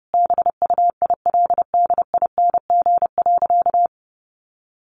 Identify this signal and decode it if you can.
Morse code, heard as a keyed tone, about 30 wpm, 700 Hz. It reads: BUILDING.
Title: Additional Easy Phrases 30wpm